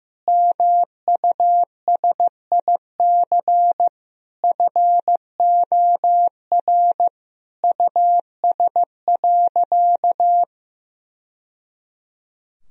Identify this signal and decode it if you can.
Morse code, heard as a keyed tone, about 15 wpm, 700 Hz. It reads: MUSIC FOR US.
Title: Sentences from Top 1000 Words - Review Part 3 - 15wpm